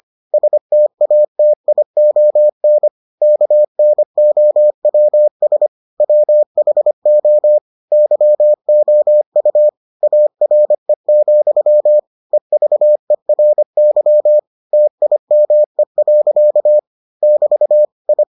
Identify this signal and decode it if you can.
Morse code, heard as a keyed tone, about 25 wpm, 600 Hz. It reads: STATION KNOWS WHO YOU ARE, EVERY TIME. <BT> I